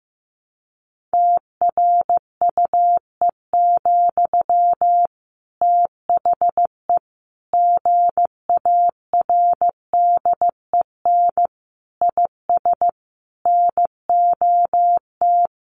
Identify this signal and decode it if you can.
Morse code, heard as a keyed tone, about 15 wpm, 700 Hz. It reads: TRUE, THE GARDEN IS NOT
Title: Sentences from Top 700 Words 15wpm